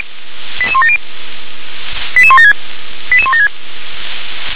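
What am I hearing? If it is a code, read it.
Five-tone selective call (ZVEI), heard as quick runs of five tones, tones of about 70 ms. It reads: 88170 88176 88176
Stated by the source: ZVEI five-tone selective call 88119 received by SDR.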